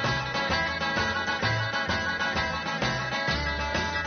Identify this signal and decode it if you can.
Emergency Alert System (SAME) tones, heard as harsh buzzing bursts. It is heard from 1.0 to 3.6 s.
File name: wrct-log hi- 04